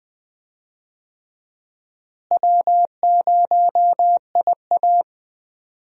Morse code, heard as a keyed tone, about 20 wpm, 700 Hz. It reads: W0IA
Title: USA Call Signs - 4 Characters - Repeated 3x - 20wpm